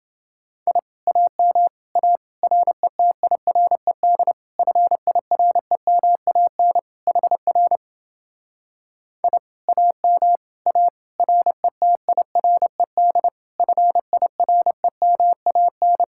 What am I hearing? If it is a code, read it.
Morse code, heard as a keyed tone, about 30 wpm, 700 Hz. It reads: I AM A RETIRED FIREMAN HR I AM A RETIRED FIREMAN